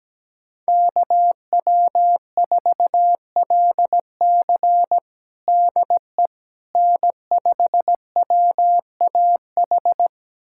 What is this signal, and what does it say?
Morse code, heard as a keyed tone, about 17 wpm, 700 Hz. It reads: KW4LC DE N5WAH
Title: USA Callsign DE Callsign - 5 Characters - 17wpm